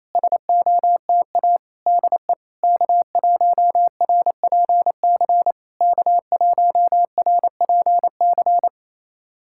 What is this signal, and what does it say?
Morse code, heard as a keyed tone, about 28 wpm, 690 Hz. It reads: SOTA DE K1RPC K1RPC
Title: SOTA QSO Practice 28wpm